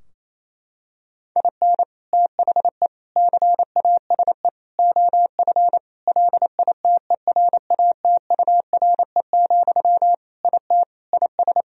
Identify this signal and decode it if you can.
Morse code, heard as a keyed tone, about 28 wpm, 700 Hz. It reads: IN THE CASE OF LITERATURE, IT IS
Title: Sentences from Top 1000 Words - Review Part 3 - 28wpm